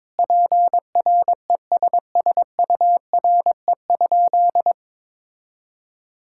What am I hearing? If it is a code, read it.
Morse code, heard as a keyed tone, about 22 wpm, 700 Hz. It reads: PRESSURE?
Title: Sentences from Top 1000 Words - Review Part 6 - 22wpm